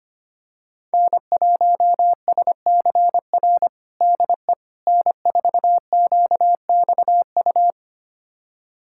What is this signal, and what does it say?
Morse code, heard as a keyed tone, about 25 wpm, 700 Hz. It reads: N1SCR DE N4QXU
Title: USA Callsign DE Callsign - 5 Characters - 25wpm